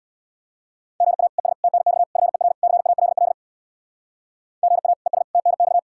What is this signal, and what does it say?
Morse code, heard as a keyed tone, about 65 wpm, 700 Hz. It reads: GA ES TNX FER CALL GA ES TNX